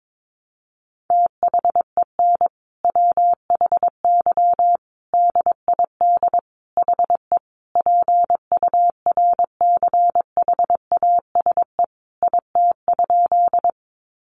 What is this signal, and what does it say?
Morse code, heard as a keyed tone, about 22 wpm, 700 Hz. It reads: THEN WHY DID HE PURCHASE IT?
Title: Sentences from Top 800 Words 22wpm